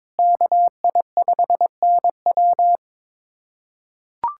Morse code, heard as a keyed tone, about 21 wpm, 700 Hz. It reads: KI5NW
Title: USA Callsign DE Callsign 22wpm